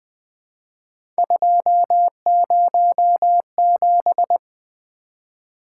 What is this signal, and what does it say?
Morse code, heard as a keyed tone, about 20 wpm, 700 Hz. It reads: 207
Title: Three Letter-Number Combinations - 20wpm